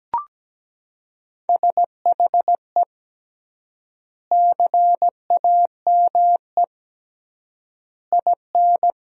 Morse code, heard as a keyed tone, about 17 wpm, 700 Hz. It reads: SHE CAME IN
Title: 3x Word-Spacing - Sentences from Top 300 Words - 17wpm